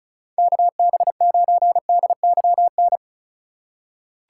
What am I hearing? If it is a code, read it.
Morse code, heard as a keyed tone, about 36 wpm, 700 Hz. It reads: KB9DYN
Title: USA Call Signs - 3x Character Spacing - 35wpm